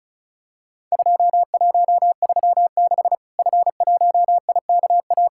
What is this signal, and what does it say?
Morse code, heard as a keyed tone, about 36 wpm, 700 Hz. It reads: 2136 F1IKA
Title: Worked All Europe DX Contest - QTC Exchange - 35wpm